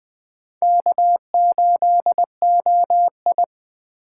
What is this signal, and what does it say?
Morse code, heard as a keyed tone, about 21 wpm, 700 Hz. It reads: K8OI